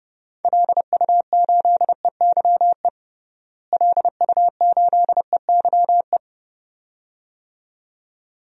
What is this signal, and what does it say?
Morse code, heard as a keyed tone, about 32 wpm, 700 Hz. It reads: LU8EYE LU8EYE
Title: DXCC-100 Call Signs - 6 Characters - Repeated 2x - 30wpm